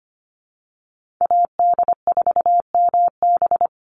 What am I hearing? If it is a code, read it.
Morse code, heard as a keyed tone, about 25 wpm, 700 Hz. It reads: AD4MB
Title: USA Call Signs 25wpm